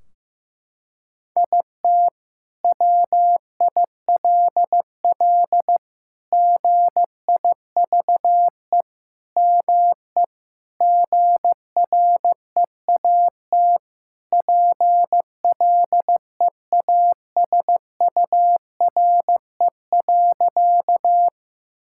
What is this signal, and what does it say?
Morse code, heard as a keyed tone, about 16 wpm, 700 Hz. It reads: IT WILL GIVE ME GREAT PLEASURE.
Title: Sentences from Top 1600 Words 15wpm